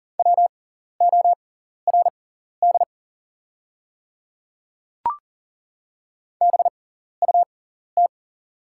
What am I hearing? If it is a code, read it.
Morse code, heard as a keyed tone, about 40 wpm, spaced as for 14 wpm, 690 Hz. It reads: WORD BUT
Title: 6x Character Spacing - Top 100 Words - 40wpm